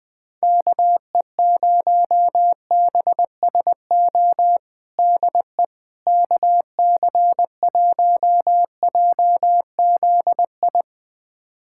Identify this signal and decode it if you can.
Morse code, heard as a keyed tone, about 20 wpm, 700 Hz. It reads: KE0BSO DE KC1JZI
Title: USA Callsign DE Callsign - 6 Characters - 20wpm